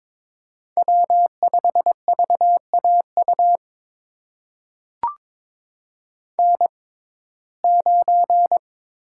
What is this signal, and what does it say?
Morse code, heard as a keyed tone, about 22 wpm, spaced as for 16 wpm, 700 Hz. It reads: W5VAU N9
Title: USA Call Signs - 5 Characters - 6x Character Spacing - 22wpm